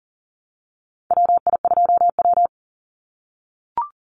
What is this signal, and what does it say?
Morse code, heard as a keyed tone, about 40 wpm, 700 Hz. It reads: WI2W
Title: NA SOTA Chaser Callsigns 40wpm